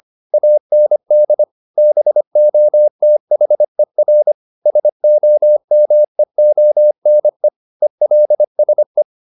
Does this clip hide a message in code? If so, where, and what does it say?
Yes: Morse code from 0.3 to 9.0 s, heard as a keyed tone, about 25 wpm, 600 Hz. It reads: AND BOTHER SOMEONE ELSE